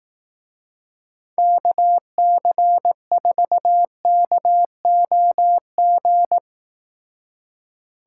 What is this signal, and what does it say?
Morse code, heard as a keyed tone, about 18 wpm, 700 Hz. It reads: KC4KOG
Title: Speed Racing - USA 2x3 Call Signs 15wpm